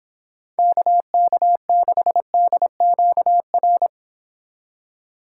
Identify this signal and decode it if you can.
Morse code, heard as a keyed tone, about 25 wpm, 700 Hz. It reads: KK6DQR